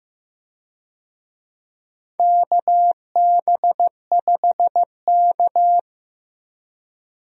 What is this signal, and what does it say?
Morse code, heard as a keyed tone, about 15 wpm, 700 Hz. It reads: KB5K
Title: USA Call Signs - 4 Characters - Repeated 3x - 15wpm